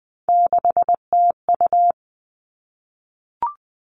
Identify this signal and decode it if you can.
Morse code, heard as a keyed tone, about 20 wpm, 700 Hz. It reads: 6TU
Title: All Cut Numbers - 3-Digit 20wpm